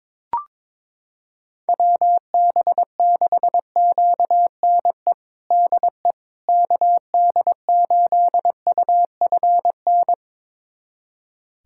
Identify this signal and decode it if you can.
Morse code, heard as a keyed tone, about 22 wpm, 700 Hz. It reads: WB6QNE DE KD8UFN
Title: USA Callsign DE Callsign - 6 Characters - 22wpm